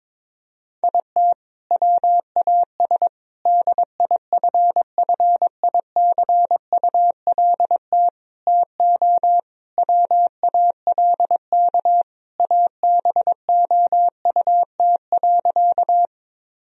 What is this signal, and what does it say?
Morse code, heard as a keyed tone, about 22 wpm, 700 Hz. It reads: IT WAS DIFFICULT TO WALK ABOUT.
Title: Sentences from Top 1000 Words - Review Part 1 - 22wpm